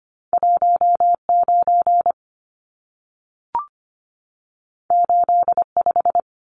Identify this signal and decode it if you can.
Morse code, heard as a keyed tone, about 25 wpm, 700 Hz. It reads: 19 85
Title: Numbers - 2-Digit 25wpm